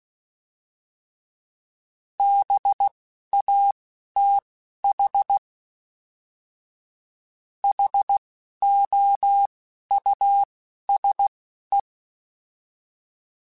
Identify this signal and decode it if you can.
Morse code, heard as a keyed tone, about 16 wpm, spaced as for 11 wpm, 800 Hz. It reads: BATH HOUSE